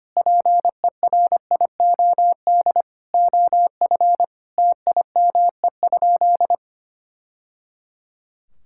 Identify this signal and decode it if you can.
Morse code, heard as a keyed tone, about 25 wpm, 700 Hz. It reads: PERIOD OF TIME?
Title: Sentences from Top 1000 Words - Review Part 5 - 25wpm